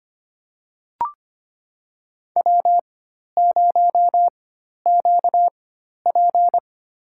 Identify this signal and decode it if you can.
Morse code, heard as a keyed tone, about 25 wpm, spaced as for 12 wpm, 700 Hz. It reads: W0QP
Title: USA Call Signs - 4 Characters - 4x Character Spacing - 25wpm